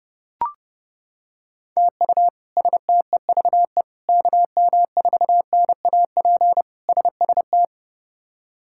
Morse code, heard as a keyed tone, about 30 wpm, 700 Hz. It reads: TU STEVE KM4NAP SST
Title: K1USN SST QSO Practice - 30wpm